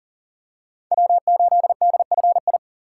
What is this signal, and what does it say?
Morse code, heard as a keyed tone, about 40 wpm, 700 Hz. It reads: W8DFI
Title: USA Call Signs - 3x Character Spacing - 40wpm